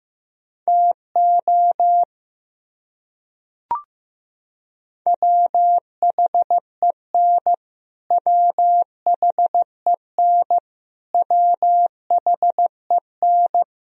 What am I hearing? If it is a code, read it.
Morse code, heard as a keyed tone, about 15 wpm, 700 Hz. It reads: TO WHEN WHEN WHEN
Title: Top 100 Words Repeated 15wpm